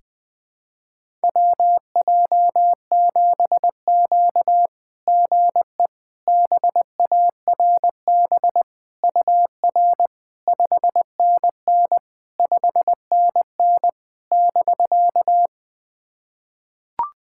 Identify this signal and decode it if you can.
Morse code, heard as a keyed tone, about 21 wpm, 690 Hz. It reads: WJ7Q GE BARB UR 5NN 5NN <BK>
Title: SOTA QSO Practice 20wpm